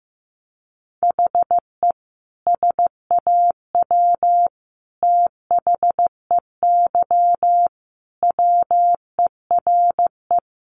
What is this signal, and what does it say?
Morse code, heard as a keyed tone, about 15 wpm, 700 Hz. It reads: HE SAW THEY WERE
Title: Sentences from Top 700 Words 15wpm